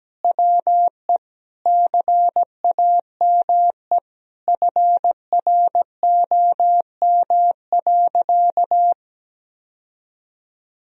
Morse code, heard as a keyed tone, about 17 wpm, 700 Hz. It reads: WE CAME FROM.
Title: Sentences from Top 1000 Words - Review Part 4 - 17wpm